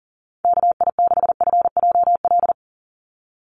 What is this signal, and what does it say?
Morse code, heard as a keyed tone, about 40 wpm, 700 Hz. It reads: KI6FJL
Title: USA Call Signs 40wpm